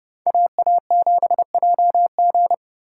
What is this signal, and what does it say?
Morse code, heard as a keyed tone, about 36 wpm, 700 Hz. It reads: AA7JG
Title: Speed Racing - USA 2x2 Call Signs 30wpm